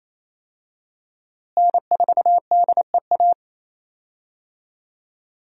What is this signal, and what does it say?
Morse code, heard as a keyed tone, about 28 wpm, 700 Hz. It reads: N4DEA
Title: USA Call Signs - 5 Characters - Repeated 3x - 28wpm